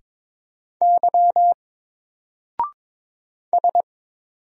Morse code, heard as a keyed tone, about 22 wpm, 700 Hz. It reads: Y S
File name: Single-Letter-Number-22wpm